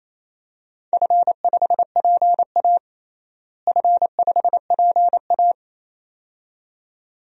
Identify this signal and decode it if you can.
Morse code, heard as a keyed tone, about 28 wpm, 700 Hz. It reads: F5PA F5PA
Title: DXCC-100 Call Signs - 4 Characters - Repeated 2x - 28wpm